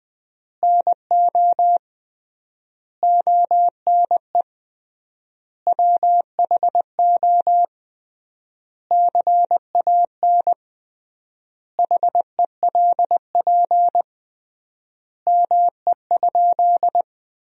Morse code, heard as a keyed tone, about 20 wpm, 700 Hz. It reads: NO ONE WHO CAN HELP ME?
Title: 3x Word-Spacing - Sentences from Top 200 Words - 20wpm